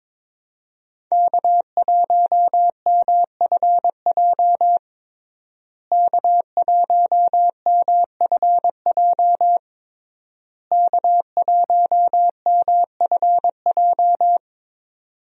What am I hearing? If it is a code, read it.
Morse code, heard as a keyed tone, about 22 wpm, 700 Hz. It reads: K1MFJ K1MFJ K1MFJ
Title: USA Call Signs - 5 Characters - Repeated 3x - 22wpm